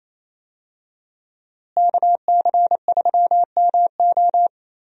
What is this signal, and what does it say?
Morse code, heard as a keyed tone, about 28 wpm, 700 Hz. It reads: KC3MO
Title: USA Call Signs - 5 Characters - Repeated 3x - 28wpm